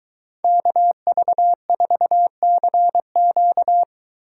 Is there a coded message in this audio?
Yes: Morse code from 0.4 to 3.8 s, heard as a keyed tone, about 22 wpm, 700 Hz. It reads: KV4CQ